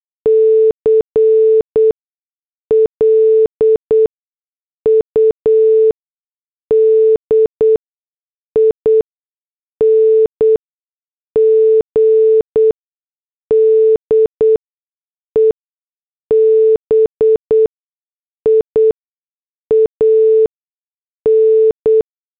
Morse code, heard as a keyed tone, about 8 wpm, spaced as for 6 wpm, 440 Hz. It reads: CLUDING DEBIAN